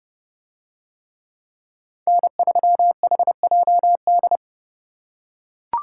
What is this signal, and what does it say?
Morse code, heard as a keyed tone, about 28 wpm, 700 Hz. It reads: N3HJD